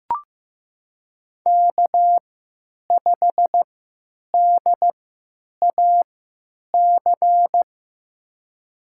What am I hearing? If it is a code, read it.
Morse code, heard as a keyed tone, about 16 wpm, spaced as for 9 wpm, 700 Hz. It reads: K5DAC
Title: USA Call Signs - 5 Characters - 3x Character Spacing - 15wpm